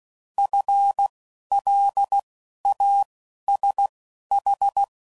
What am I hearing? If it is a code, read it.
Morse code, heard as a keyed tone, about 16 wpm, 800 Hz. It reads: F L A S H